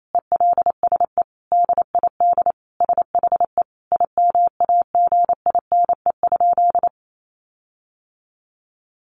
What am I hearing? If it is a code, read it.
Morse code, heard as a keyed tone, about 28 wpm, 700 Hz. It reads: ELSE DID SHE IMAGINE?
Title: Sentences from Top 1400 Words 28wpm